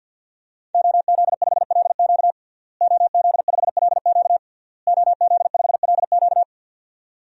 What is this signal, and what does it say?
Morse code, heard as a keyed tone, about 49 wpm, 700 Hz. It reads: OZ5LQ OZ5LQ OZ5LQ
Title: DXCC-100 Call Signs - 5 Characters - Repeated 3x - 50wpm